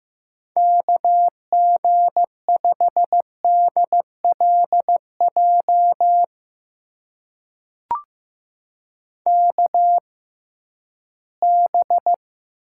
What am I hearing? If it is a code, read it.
Morse code, heard as a keyed tone, about 15 wpm, spaced as for 11 wpm, 700 Hz. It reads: KG5DLJ KB
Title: USA Call Signs - 6 Characters - 6x Character Spacing - 15wpm